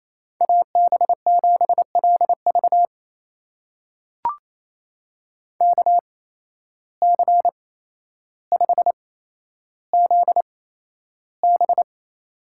Morse code, heard as a keyed tone, about 28 wpm, spaced as for 12 wpm, 700 Hz. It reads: AB7LV KC5ZB
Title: USA Call Signs - 5 Characters - 8x Character Spacing - 28wpm